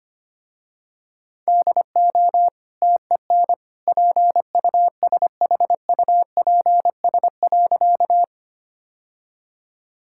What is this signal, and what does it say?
Morse code, heard as a keyed tone, about 25 wpm, 700 Hz. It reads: DO TEN PUSHUPS.
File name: Easy Sentences with Hints - 25wpm